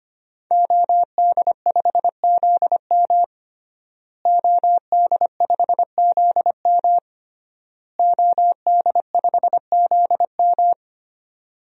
Morse code, heard as a keyed tone, about 25 wpm, 700 Hz. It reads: OD5ZM OD5ZM OD5ZM
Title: DXCC-100 Call Signs - 5 Characters - Repeated 3x - 25wpm